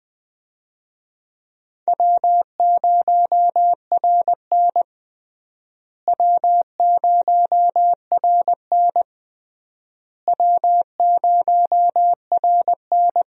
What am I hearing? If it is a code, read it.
Morse code, heard as a keyed tone, about 21 wpm, 700 Hz. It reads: W0RN W0RN W0RN
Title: USA Call Signs - 4 Characters - Repeated 3x - 20wpm